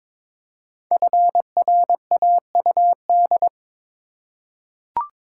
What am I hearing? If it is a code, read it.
Morse code, heard as a keyed tone, about 22 wpm, 700 Hz. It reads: FRAUD
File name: 5 Letter Words - First 3 Letters Given - 22wpm